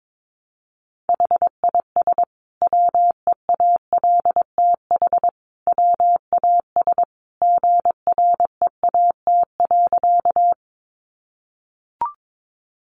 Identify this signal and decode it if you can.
Morse code, heard as a keyed tone, about 22 wpm, 700 Hz. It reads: HIS WEALTH WAS GREAT.
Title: Sentences from Top 1200 Words 22wpm